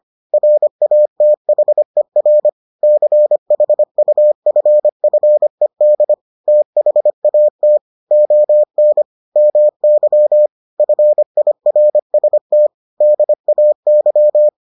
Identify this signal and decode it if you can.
Morse code, heard as a keyed tone, about 25 wpm, 600 Hz. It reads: RATHER CHUFFED THAT ON MY FIRST DAY